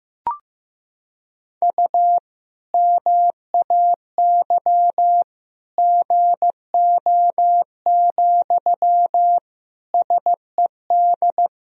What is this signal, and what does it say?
Morse code, heard as a keyed tone, about 15 wpm, 700 Hz. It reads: U MAY GO, SED
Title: Sentences from Top 400 Words and Abbreviations 15wpm